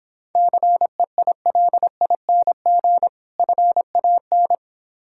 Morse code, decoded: CEILING FAN

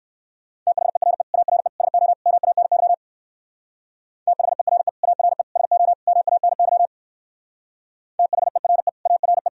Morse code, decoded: THERE ARE SO MANY THERE ARE SO MANY THERE ARE

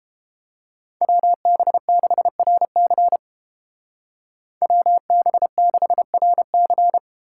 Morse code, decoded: WB6RC WB6RC

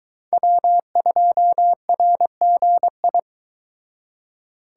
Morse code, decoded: W2RGI

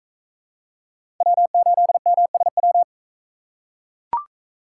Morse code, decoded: W8MSW

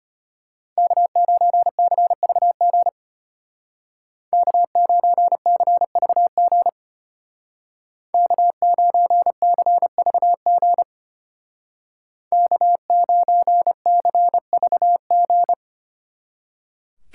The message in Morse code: K9CVG K9CVG K9CVG K9CVG

